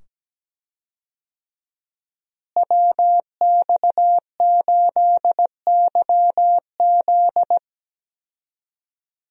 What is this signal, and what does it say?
Morse code, heard as a keyed tone, about 17 wpm, 700 Hz. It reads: WX8YZ
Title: USA Call Signs - 5 Characters - Repeated 2x - 17wpm